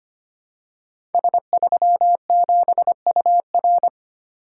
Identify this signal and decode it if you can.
Morse code, heard as a keyed tone, about 25 wpm, 700 Hz. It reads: S37UR
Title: Code Groups - 5 Letter:Numbers - 25wpm